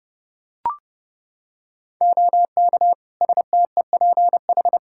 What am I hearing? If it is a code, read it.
Morse code, heard as a keyed tone, about 32 wpm, 700 Hz. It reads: OK STEPH
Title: Practice QSO - Exchange 2 - Info - 30wpm